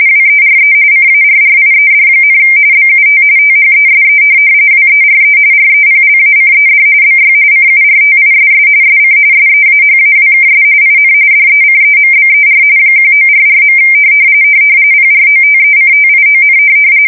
RTTY, about 45.45 baud. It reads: WERE CARRIED, BUT THESE WERE REPLACED BY CURTISS SC SEAHAWKS IN DECEMBER 1944. PROTECTION
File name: fsk170 / USS-Missouri